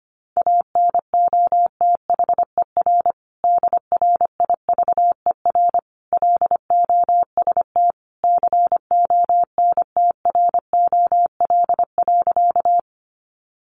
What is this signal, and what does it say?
Morse code, heard as a keyed tone, about 25 wpm, 700 Hz. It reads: ANOTHER DRIVER LOST CONTROL.
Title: Sentences from Top 1400 Words 25wpm